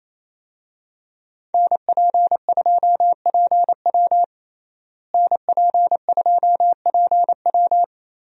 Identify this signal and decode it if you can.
Morse code, heard as a keyed tone, about 28 wpm, 700 Hz. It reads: NP2PW NP2PW
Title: USA Call Signs - 5 Characters - Repeated 3x - 28wpm